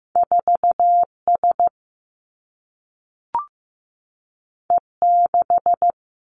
Morse code, heard as a keyed tone, about 15 wpm, 700 Hz. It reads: 4 S E 6
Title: Two Challenging Letter-Number Combinations 15wpm